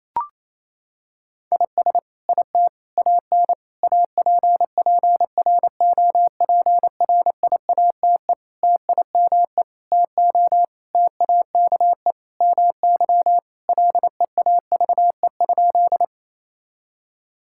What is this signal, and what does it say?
Morse code, heard as a keyed tone, about 28 wpm, 700 Hz. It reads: IS IT AN APPROPRIATE TIME TO TAKE MY LEAVE?
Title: Sentences from Top 1000 Words - Review Part 5 - 28wpm